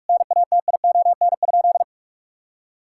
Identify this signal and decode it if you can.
Morse code, heard as a keyed tone, about 41 wpm, 700 Hz. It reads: NATION?
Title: Sentences from Top 1000 Words - Review Part 6 - 45wpm